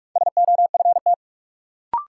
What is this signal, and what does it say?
Morse code, heard as a keyed tone, about 42 wpm, 700 Hz. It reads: SOFT
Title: Sentences from Top 1500 Words - New Word Drill - 45wpm